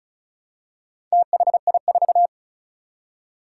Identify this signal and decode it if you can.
Morse code, heard as a keyed tone, about 36 wpm, 700 Hz. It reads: THI4
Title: Four Challenging Letter-Number Combinations 35wpm